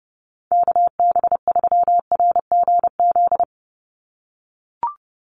Morse code, heard as a keyed tone, about 30 wpm, 700 Hz. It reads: KB3RGZ
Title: USA Call Signs 30wpm